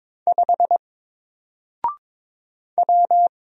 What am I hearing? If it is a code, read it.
Morse code, heard as a keyed tone, about 22 wpm, 700 Hz. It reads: 5 W
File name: Single-Letter-Number-22wpm